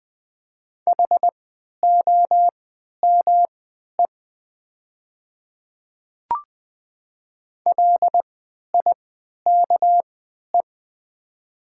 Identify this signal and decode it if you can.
Morse code, heard as a keyed tone, about 20 wpm, spaced as for 11 wpm, 700 Hz. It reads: HOME LIKE
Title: 3x Character Spacing - Top 200 Words - 20wpm